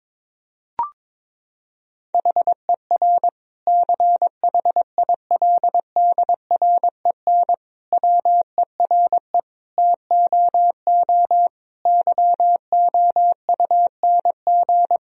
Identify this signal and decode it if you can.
Morse code, heard as a keyed tone, about 22 wpm, 700 Hz. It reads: HER CHILDREN WERE TOO YOUNG
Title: Sentences from Top 1000 Words - Review Part 4 - 22wpm